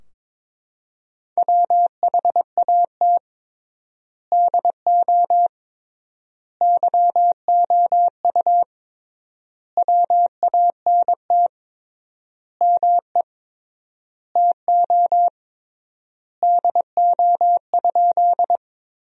Morse code, heard as a keyed tone, about 22 wpm, 700 Hz. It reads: WHAT DO YOU WANT ME TO DO?
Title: 3x Word-Spacing - Sentences from Top 200 Words - 22wpm